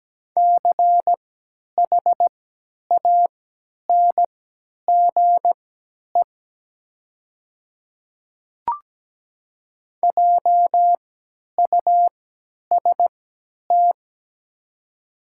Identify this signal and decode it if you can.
Morse code, heard as a keyed tone, about 17 wpm, spaced as for 10 wpm, 700 Hz. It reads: CHANGE JUST
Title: 3x Character Spacing - Top 200 Words - 17wpm